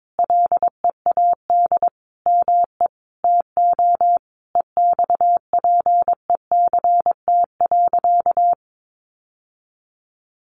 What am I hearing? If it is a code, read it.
Morse code, heard as a keyed tone, about 22 wpm, 700 Hz. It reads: LEAD ME TO EXPECT.
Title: Sentences from Top 1200 Words 22wpm